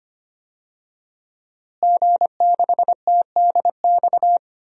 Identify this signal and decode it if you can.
Morse code, heard as a keyed tone, about 25 wpm, 700 Hz. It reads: G6TDX
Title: DXCC-100 Call Signs - 5 Characters - Repeated 2x - 25wpm